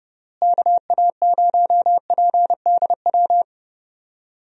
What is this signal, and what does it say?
Morse code, heard as a keyed tone, about 30 wpm, 700 Hz. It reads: KA0PDW
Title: USA Call Signs - 6 Characters - Repeated 3x - 30wpm